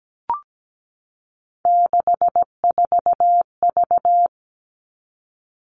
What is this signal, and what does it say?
Morse code, heard as a keyed tone, about 17 wpm, 700 Hz. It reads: 64V
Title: All Cut Numbers - 3-Digit 17wpm